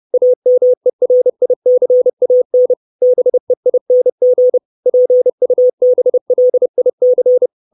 Morse code, heard as a keyed tone, about 30 wpm, 500 Hz. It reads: AMERICAN BEING PUBLIC
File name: 632 Words 30WPM